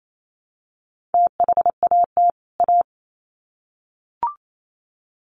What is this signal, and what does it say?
Morse code, heard as a keyed tone, about 28 wpm, 700 Hz. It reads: THAT A
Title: Sets of 2 Words 28wpm